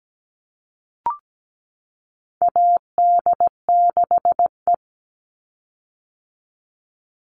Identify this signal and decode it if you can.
Morse code, heard as a keyed tone, about 17 wpm, 700 Hz. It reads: AD6E
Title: Call Signs - 17wpm